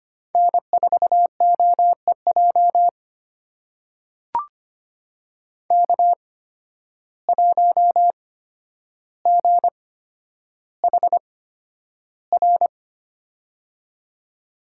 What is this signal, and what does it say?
Morse code, heard as a keyed tone, about 25 wpm, spaced as for 11 wpm, 700 Hz. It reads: N4OEJ K1GHR
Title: USA Call Signs - 5 Characters - 8x Character Spacing - 25wpm